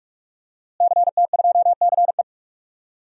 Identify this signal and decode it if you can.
Morse code, heard as a keyed tone, about 46 wpm, 700 Hz. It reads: CT2CE
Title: DXCC-100 Call Signs - 5 Characters - Repeated 3x - 45wpm